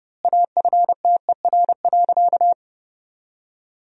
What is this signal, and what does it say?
Morse code, heard as a keyed tone, about 32 wpm, 700 Hz. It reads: AFTER.
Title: 3x Word-Spacing - Sentences from Top 500 Words - 30wpm